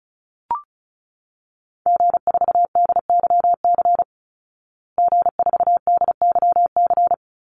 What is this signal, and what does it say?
Morse code, heard as a keyed tone, about 34 wpm, 700 Hz. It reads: G4DYC G4DYC
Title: Call Signs Repeated 3x - 35wpm